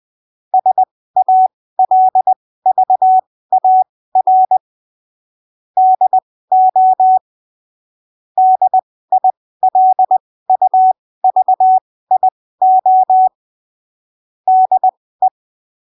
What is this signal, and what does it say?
Morse code, heard as a keyed tone, about 20 wpm, spaced as for 15 wpm, 750 Hz. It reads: SALVAR DO DILUVIO DE
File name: 20ppm